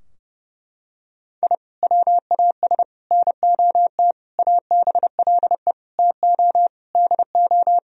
Morse code, heard as a keyed tone, about 30 wpm, 700 Hz. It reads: I WAS NOT ABLE TO DO
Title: Sentences from Top 1000 Words - Review Part 2 - 30wpm